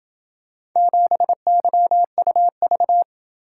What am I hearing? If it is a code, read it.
Morse code, heard as a keyed tone, about 27 wpm, 700 Hz. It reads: 7YUV